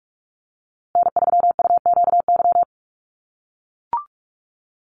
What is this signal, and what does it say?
Morse code, heard as a keyed tone, about 44 wpm, 700 Hz. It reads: N3UQY
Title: USA Call Signs Repeated 3x 45wpm